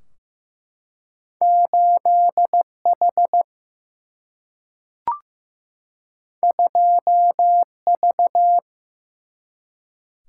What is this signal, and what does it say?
Morse code, heard as a keyed tone, about 15 wpm, 700 Hz. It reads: 8H 2V
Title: Single-Letter Single-Number 15wpm